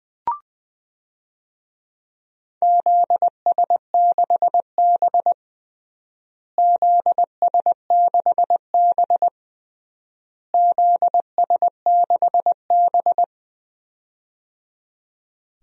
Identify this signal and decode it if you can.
Morse code, heard as a keyed tone, about 20 wpm, 700 Hz. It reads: ZS6B ZS6B ZS6B
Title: DXCC-100 Call Signs - 4 Characters - Repeated 3x - 20wpm